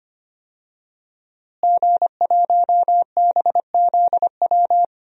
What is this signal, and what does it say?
Morse code, heard as a keyed tone, about 25 wpm, 700 Hz. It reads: G1BZW
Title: DXCC-100 Call Signs - 5 Characters - Repeated 3x - 25wpm